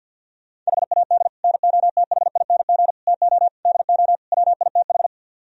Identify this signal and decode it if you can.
Morse code, heard as a keyed tone, about 49 wpm, 690 Hz. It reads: HAD NOTHING TO DO WITH
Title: Speed-Racing - Sets of 5 Words 50wpm